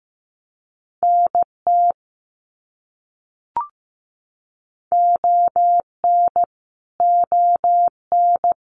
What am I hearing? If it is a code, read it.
Morse code, heard as a keyed tone, about 15 wpm, 700 Hz. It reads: NT ON ON
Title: Canadian Provinces and Territories - Repeated 3x - 15wpm